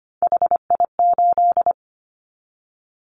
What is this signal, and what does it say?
Morse code, heard as a keyed tone, about 25 wpm, 700 Hz. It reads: HI8
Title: Callsign Prefixes for Top 100 DXCC - 25wpm